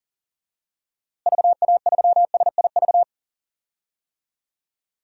Morse code, heard as a keyed tone, about 40 wpm, 700 Hz. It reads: VA3SIV
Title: DXCC-100 Call Signs - 6 Characters - Repeated 2x - 40wpm